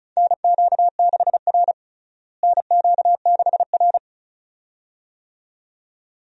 Morse code, heard as a keyed tone, about 36 wpm, 700 Hz. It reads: NQ6R NQ6R
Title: USA Call Signs - 4 Characters - Repeated 2x - 35wpm